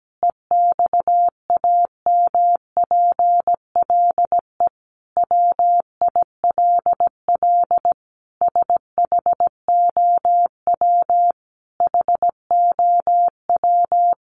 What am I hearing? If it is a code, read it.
Morse code, heard as a keyed tone, about 17 wpm, 700 Hz. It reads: EXAMPLE WILL SHOW HOW
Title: Sentences from Top 1500 Words 17wpm